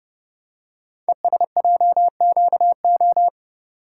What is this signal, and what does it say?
Morse code, heard as a keyed tone, about 30 wpm, 700 Hz. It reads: ESJQO
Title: Code Groups - 5 Letters - 30wpm